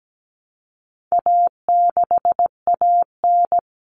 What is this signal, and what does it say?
Morse code, heard as a keyed tone, about 17 wpm, 700 Hz. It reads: A6AN
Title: Common Cut Numbers - 4-Digit 17wpm